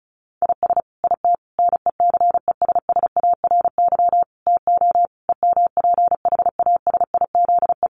Morse code, heard as a keyed tone, about 34 wpm, 700 Hz. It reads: IS IT NECESSARY TO EMPHASIZE
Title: Sentences from Top 1500 Words 35wpm